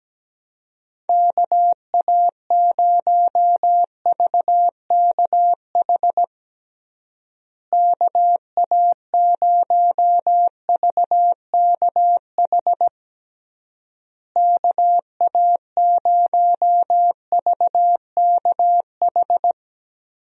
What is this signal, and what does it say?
Morse code, heard as a keyed tone, about 17 wpm, 700 Hz. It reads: KA0VKH KA0VKH KA0VKH
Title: USA Call Signs - 6 Characters - Repeated 3x - 17wpm